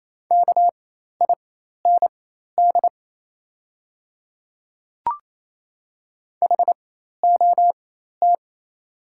Morse code, heard as a keyed tone, about 27 wpm, spaced as for 13 wpm, 700 Hz. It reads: KIND HOT
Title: 4x Character Spacing - Top 300 Words - 28wpm